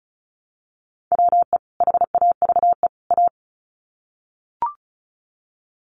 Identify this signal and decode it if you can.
Morse code, heard as a keyed tone, about 34 wpm, 700 Hz. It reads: WE HAVE A
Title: Sets of 3 Words 35wpm